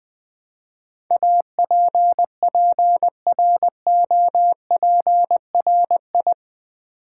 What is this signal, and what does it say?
Morse code, heard as a keyed tone, about 20 wpm, 700 Hz. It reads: APPROPRI
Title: Build A Word - 4 Letters At A Time - 11-Letter Words - 20wpm